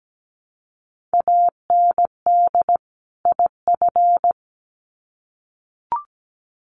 Morse code, heard as a keyed tone, about 17 wpm, 700 Hz. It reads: AND IF
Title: Sets of 2 Words 17wpm